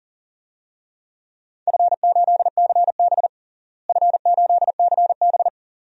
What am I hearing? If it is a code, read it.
Morse code, heard as a keyed tone, about 37 wpm, 700 Hz. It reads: F8CB F8CB
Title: DXCC-100 Call Signs - 4 Characters - Repeated 2x - 40wpm